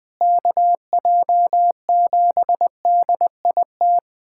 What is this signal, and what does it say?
Morse code, heard as a keyed tone, about 21 wpm, 700 Hz. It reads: KJ7DIT
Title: USA Callsign DE Callsign - 6 Characters - 20wpm